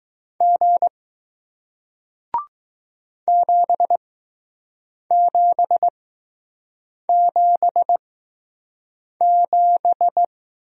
Morse code, heard as a keyed tone, about 20 wpm, spaced as for 6 wpm, 700 Hz. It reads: G 7777